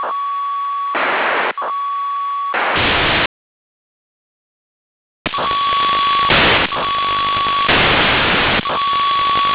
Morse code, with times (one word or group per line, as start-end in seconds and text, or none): none